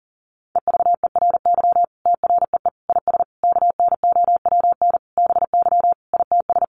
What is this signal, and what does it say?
Morse code, heard as a keyed tone, about 40 wpm, 700 Hz. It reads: EVERY TREE IS KNOWN BY ITS
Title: Sentences from Top 1300 Words 40wpm